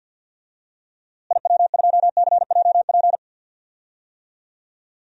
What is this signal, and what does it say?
Morse code, heard as a keyed tone, about 53 wpm, 700 Hz. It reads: IW2CJP
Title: DXCC-100 Call Signs - 6 Characters - Repeated 3x - 50wpm